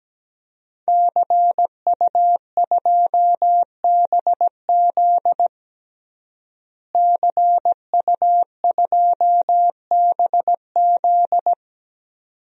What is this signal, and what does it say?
Morse code, heard as a keyed tone, about 17 wpm, 700 Hz. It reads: CU2BZ CU2BZ
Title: DXCC-100 Call Signs - 5 Characters - Repeated 2x - 17wpm